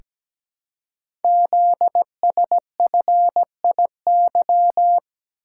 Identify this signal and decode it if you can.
Morse code, heard as a keyed tone, about 17 wpm, 700 Hz. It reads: ZSFIY
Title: Code Groups - 5 Letters - 17wpm